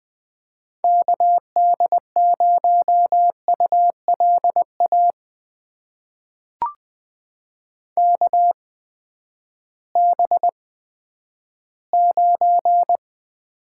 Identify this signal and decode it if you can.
Morse code, heard as a keyed tone, about 20 wpm, spaced as for 11 wpm, 700 Hz. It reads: KD0ULA KB9
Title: USA Call Signs - 6 Characters - 8x Character Spacing - 20wpm